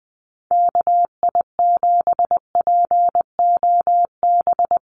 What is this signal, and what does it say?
Morse code, heard as a keyed tone, about 20 wpm, 700 Hz. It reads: KI7POB
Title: USA Call Signs 20wpm